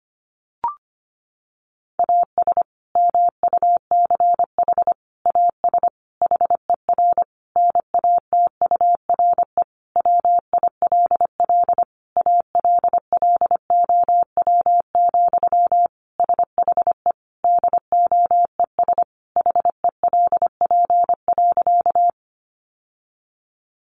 Morse code, heard as a keyed tone, about 25 wpm, 700 Hz. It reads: AS MUCH AS HER NATURE WILL ALLOW, SHE DOES HELP.
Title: Sentences from Top 700 Words 25wpm